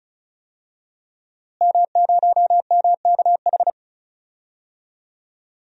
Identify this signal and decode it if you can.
Morse code, heard as a keyed tone, about 36 wpm, 700 Hz. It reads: M0MKH
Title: DXCC-100 Call Signs - 5 Characters - Repeated 2x - 35wpm